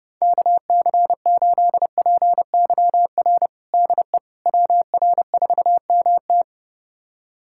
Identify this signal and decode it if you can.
Morse code, heard as a keyed tone, about 30 wpm, 700 Hz. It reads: KC8PYR DE WR4MT
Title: USA Callsign DE Callsign 30wpm